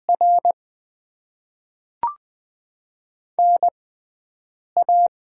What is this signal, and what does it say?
Morse code, heard as a keyed tone, about 20 wpm, spaced as for 7 wpm, 700 Hz. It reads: R NA
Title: Top 20 QSO Elements - 6x Character Spacing - 20wpm